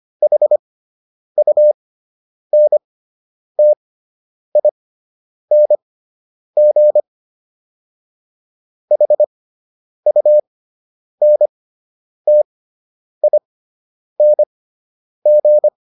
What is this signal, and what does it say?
Morse code, heard as a keyed tone, about 25 wpm, spaced as for 9 wpm, 600 Hz. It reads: HUNTING HUNTING